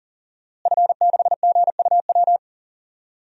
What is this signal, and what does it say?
Morse code, heard as a keyed tone, about 40 wpm, 700 Hz. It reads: F6GUW
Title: DXCC-100 Call Signs - 5 Characters - Repeated 2x - 40wpm